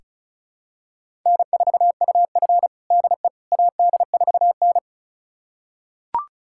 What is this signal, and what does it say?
Morse code, heard as a keyed tone, about 36 wpm, 700 Hz. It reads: N4UF DE AD4N
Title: USA Callsign DE Callsign - 4 Characters - 35wpm